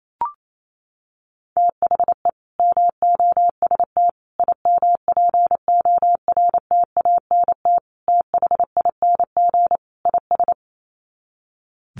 Morse code, heard as a keyed tone, about 28 wpm, 700 Hz. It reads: THE MOST IMPORTANT THING IS
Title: Sets of 5 Words Encore 28wpm